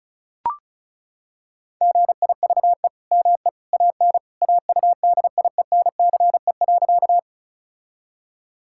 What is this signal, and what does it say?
Morse code, heard as a keyed tone, about 36 wpm, 700 Hz. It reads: GIVE ME AN AUDIENCE.
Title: Sentences from Top 1000 Words - Review Part 5 - 35wpm